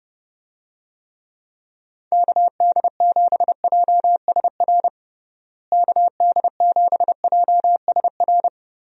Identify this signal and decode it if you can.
Morse code, heard as a keyed tone, about 30 wpm, 700 Hz. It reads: KD7JSR KD7JSR
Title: USA Call Signs - 6 Characters - Repeated 3x - 30wpm